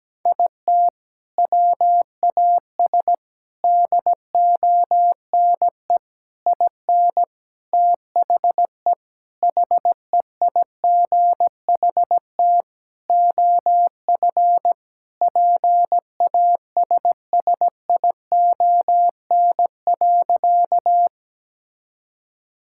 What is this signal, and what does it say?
Morse code, heard as a keyed tone, about 17 wpm, 700 Hz. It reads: IT WAS DONE IN THE HEIGHT OF PASSION.
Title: Sentences from Top 1600 Words 17wpm